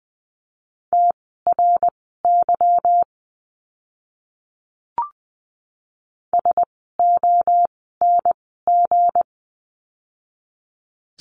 Morse code, heard as a keyed tone, about 20 wpm, spaced as for 15 wpm, 700 Hz. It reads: TRY SONG
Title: 160 Lesson 26f - Letter g - Words 20wpm